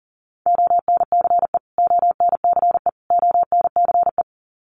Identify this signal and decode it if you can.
Morse code, heard as a keyed tone, about 40 wpm, 700 Hz. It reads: ONCE ONCE ONCE
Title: Top 300 Words Repeated 40wpm